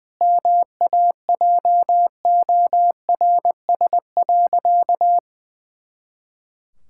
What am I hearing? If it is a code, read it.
Morse code, heard as a keyed tone, about 20 wpm, 700 Hz. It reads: MAJORS.
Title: Pangram Challenge v2 - 20wpm